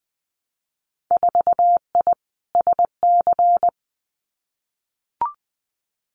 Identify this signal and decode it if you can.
Morse code, heard as a keyed tone, about 20 wpm, 700 Hz. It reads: 4I SC
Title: Winter Field Day Exchanges 20wpm